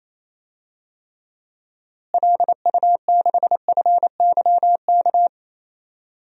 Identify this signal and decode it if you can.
Morse code, heard as a keyed tone, about 27 wpm, 700 Hz. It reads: LU6FYK
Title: DXCC-100 Call Signs - 6 Characters - Repeated 3x - 28wpm